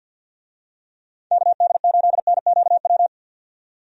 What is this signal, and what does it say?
Morse code, heard as a keyed tone, about 50 wpm, 700 Hz. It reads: KD8NQW
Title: USA Call Signs - 6 Characters - 50wpm